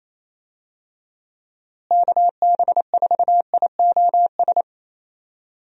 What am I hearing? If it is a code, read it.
Morse code, heard as a keyed tone, about 27 wpm, 700 Hz. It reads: KB4IOS
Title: USA Call Signs - 6 Characters - Repeated 2x - 28wpm